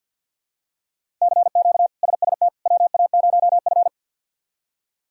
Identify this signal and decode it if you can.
Morse code, heard as a keyed tone, about 49 wpm, 700 Hz. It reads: CQ SST WA0F
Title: K1USN SST QSO Practice - 50wpm